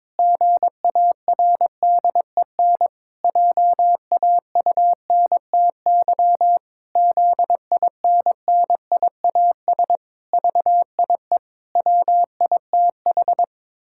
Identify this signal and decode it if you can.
Morse code, heard as a keyed tone, about 22 wpm, 700 Hz. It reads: GARDEN JAUNTY ZINNIAS VIE WITH
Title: Pangram Challenge v2 - 22wpm